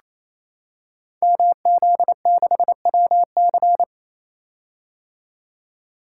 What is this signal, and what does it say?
Morse code, heard as a keyed tone, about 28 wpm, 700 Hz. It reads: MZ6WC
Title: Code Groups - 5 Letter:Numbers - 28wpm